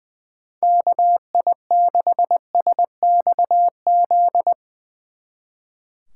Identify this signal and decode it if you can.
Morse code, heard as a keyed tone, about 22 wpm, 700 Hz. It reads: KI6SXZ
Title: Speed Racing - USA Call Signs 20wpm